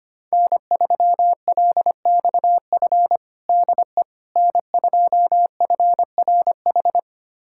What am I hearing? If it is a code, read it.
Morse code, heard as a keyed tone, about 25 wpm, 700 Hz. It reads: N3LXF DE N2FRH
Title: USA Callsign DE Callsign - 5 Characters - 25wpm